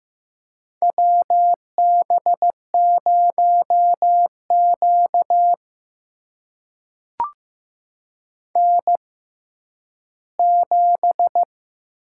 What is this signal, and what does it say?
Morse code, heard as a keyed tone, about 15 wpm, spaced as for 10 wpm, 700 Hz. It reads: WB0Q N7
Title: USA Call Signs - 4 Characters - 6x Character Spacing - 15wpm